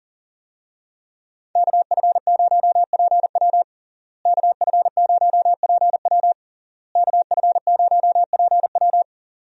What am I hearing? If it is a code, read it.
Morse code, heard as a keyed tone, about 40 wpm, 700 Hz. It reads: KF0PW KF0PW KF0PW
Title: USA Call Signs - 5 Characters - Repeated 3x - 40wpm